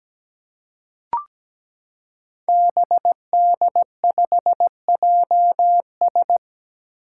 Morse code, heard as a keyed tone, about 17 wpm, 700 Hz. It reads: BD5JS
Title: Five Challenging Letter-Number Combinations 17wpm